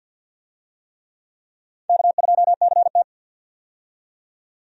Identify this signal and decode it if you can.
Morse code, heard as a keyed tone, about 52 wpm, 700 Hz. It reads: K2CT
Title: USA Call Signs - 4 Characters - 50wpm